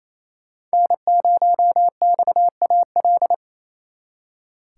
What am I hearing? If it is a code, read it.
Morse code, heard as a keyed tone, about 33 wpm, 700 Hz. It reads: N0XAL
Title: Speed Racing - USA 1x3 Call Signs 28wpm